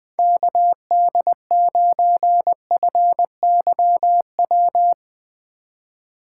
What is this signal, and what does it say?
Morse code, heard as a keyed tone, about 20 wpm, 700 Hz. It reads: KD9FYW
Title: USA Call Signs - 2x Character Spacing - 20wpm